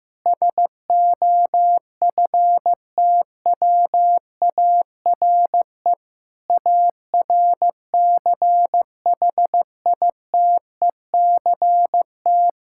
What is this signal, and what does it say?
Morse code, heard as a keyed tone, about 15 wpm, 700 Hz. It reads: SOFTWARE ARCHITECT
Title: Practice QSO - Exchange 3 - More Info - 15wpm